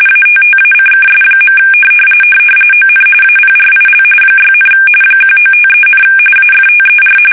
RTTY, about 45.45 baud. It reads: ATTACKS. ON APRIL 6, 1953, AFTER
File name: fsk850 / battleshipnj